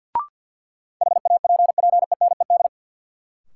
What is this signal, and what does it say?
Morse code, heard as a keyed tone, about 53 wpm, 700 Hz. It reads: HAPPENED
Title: Sentences from Top 1000 Words - Review Part 2 - New Word Drill - 50wpm